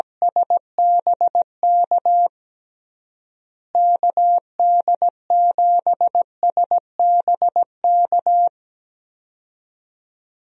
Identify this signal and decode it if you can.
Morse code, heard as a keyed tone, about 17 wpm, 700 Hz. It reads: SBK KD7SBK
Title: USA Call Signs - 6 Characters - Repeated 3x - 17wpm